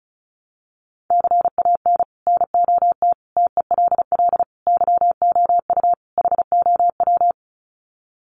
Morse code, heard as a keyed tone, about 34 wpm, 700 Hz. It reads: CAN NOT TELL YOU HOW
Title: Sets of 5 Words Encore2 35wpm